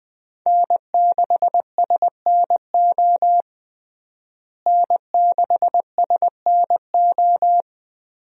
Morse code, heard as a keyed tone, about 20 wpm, 700 Hz. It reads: N6SNO N6SNO
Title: USA Call Signs - 5 Characters - Repeated 2x - 20wpm